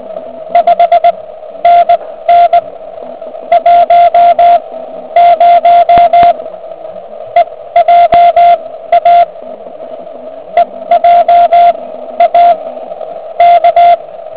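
Morse code, heard as a keyed tone, about 19 wpm, spaced as for 13 wpm, 690 Hz, with radio noise. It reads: 5NN 10 EJA EJA K
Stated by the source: Quick exchange between two contest stations on 3559 kHz, just after a QRP contest.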